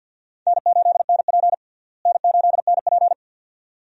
Morse code, heard as a keyed tone, about 48 wpm, 700 Hz. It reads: N8NP N8NP
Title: USA Call Signs - 4 Characters - Repeated 2x - 50wpm